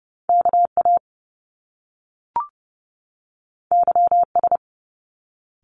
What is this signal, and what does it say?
Morse code, heard as a keyed tone, about 30 wpm, 700 Hz. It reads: KA YS
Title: Top Two Letter Combinations in Words 30wpm